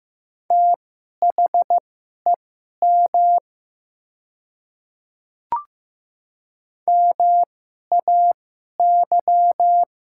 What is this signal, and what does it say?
Morse code, heard as a keyed tone, about 15 wpm, spaced as for 11 wpm, 700 Hz. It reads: THEM MAY
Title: 2x Character Spacing - Top 100 Words - 15wpm